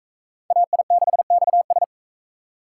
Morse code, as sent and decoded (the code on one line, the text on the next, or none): .- .. -.... -..- ...
AI6XS